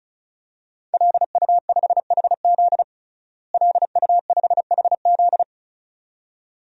Morse code, heard as a keyed tone, about 34 wpm, 700 Hz. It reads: LU5HZ LU5HZ
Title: DXCC-100 Call Signs - 5 Characters - Repeated 2x - 35wpm